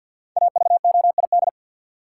Morse code, heard as a keyed tone, about 52 wpm, 700 Hz. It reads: AVOID
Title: Five Letter Words - 50wpm